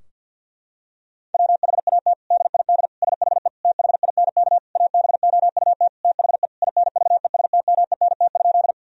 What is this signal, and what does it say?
Morse code, heard as a keyed tone, about 47 wpm, 700 Hz. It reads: WHAT DID SHE THINK ABOUT THE INVESTMENT?
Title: Sentences from Top 1000 Words - Review Part 4 - 50wpm